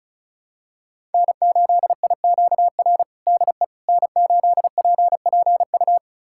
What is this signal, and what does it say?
Morse code, heard as a keyed tone, about 36 wpm, 700 Hz. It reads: N8IQR DE N8PPU
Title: USA Callsign DE Callsign - 5 Characters - 35wpm